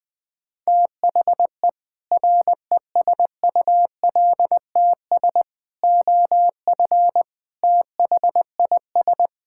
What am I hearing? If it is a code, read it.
Morse code, heard as a keyed tone, about 20 wpm, 700 Hz. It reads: THE RESULTS OF THIS
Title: Sentences with 5-Word Phrases - 20wpm